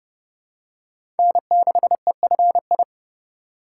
Morse code, heard as a keyed tone, about 30 wpm, 700 Hz. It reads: N6EFI
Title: USA Call Signs - 5 Characters - 2x Character Spacing - 30wpm